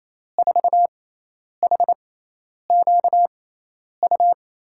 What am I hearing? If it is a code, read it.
Morse code, heard as a keyed tone, about 28 wpm, spaced as for 10 wpm, 700 Hz. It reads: 4HQU
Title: USA Call Signs - 6x Character Spacing - 28wpm